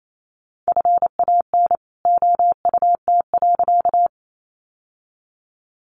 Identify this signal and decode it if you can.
Morse code, heard as a keyed tone, about 28 wpm, 700 Hz. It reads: FAN OUT.
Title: Sentences from Top 1300 Words 28wpm